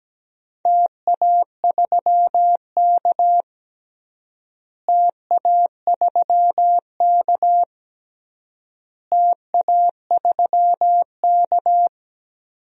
Morse code, heard as a keyed tone, about 17 wpm, 700 Hz. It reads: TA3K TA3K TA3K
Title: DXCC-100 Call Signs - 4 Characters - Repeated 3x - 17wpm